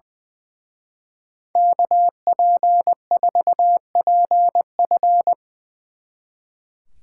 Morse code, heard as a keyed tone, about 22 wpm, 700 Hz. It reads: KP4PF
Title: Speed Racing - USA 2x2 Call Signs 20wpm